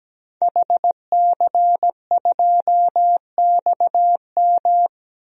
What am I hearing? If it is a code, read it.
Morse code, heard as a keyed tone, about 17 wpm, 700 Hz. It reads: HC2XM
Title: DXCC-100 Call Signs - 5 Characters - Repeated 2x - 17wpm